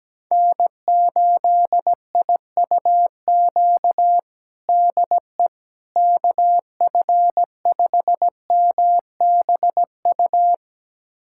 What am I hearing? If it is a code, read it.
Morse code, heard as a keyed tone, about 17 wpm, 700 Hz. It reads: N8IUQ DE KF5MBU
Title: USA Callsign DE Callsign 17wpm